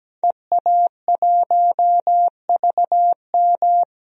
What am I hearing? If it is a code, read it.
Morse code, heard as a keyed tone, about 17 wpm, 700 Hz. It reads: EA1VM
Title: DXCC-100 Call Signs - 5 Characters - 17wpm